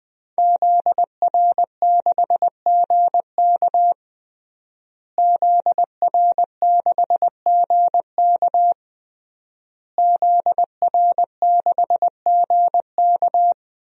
Morse code, heard as a keyed tone, about 20 wpm, 700 Hz. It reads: ZR6GK ZR6GK ZR6GK
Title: DXCC-100 Call Signs - 5 Characters - Repeated 3x - 20wpm